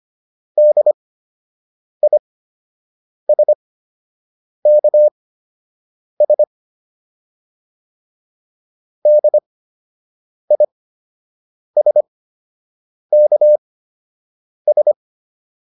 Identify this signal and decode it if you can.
Morse code, heard as a keyed tone, about 25 wpm, spaced as for 7 wpm, 600 Hz. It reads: DISKS DISKS